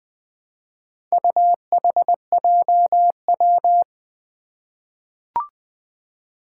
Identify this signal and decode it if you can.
Morse code, heard as a keyed tone, about 20 wpm, 700 Hz. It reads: UHJW
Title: Four Challenging Letter-Number Combinations 20wpm